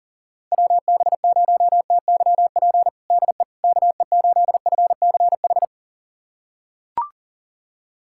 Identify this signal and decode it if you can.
Morse code, heard as a keyed tone, about 40 wpm, 700 Hz. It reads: WB0TYP DE KE8FCH
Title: USA Callsign DE Callsign - 6 Characters - 40wpm